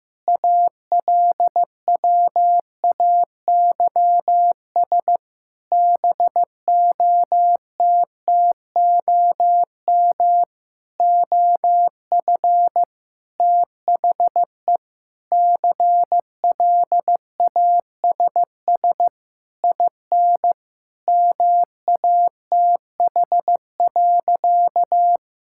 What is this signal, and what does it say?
Morse code, heard as a keyed tone, about 16 wpm, 700 Hz. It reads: ALWAYS BOTTOM OF THE CLASS IN MATH.
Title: Sentences from Top 1000 Words - Review Part 6 - 15wpm